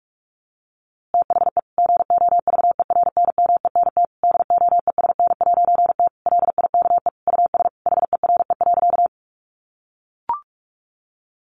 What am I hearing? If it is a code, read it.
Morse code, heard as a keyed tone, about 44 wpm, 700 Hz. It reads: THE GOVERNMENT DOESN'T LIKE US HERE.
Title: A Sound Of Thunder 45wpm